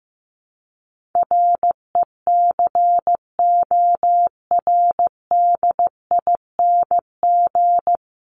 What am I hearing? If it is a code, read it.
Morse code, heard as a keyed tone, about 15 wpm, 700 Hz. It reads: RECORDING
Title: Top 1501-1750 Words 15wpm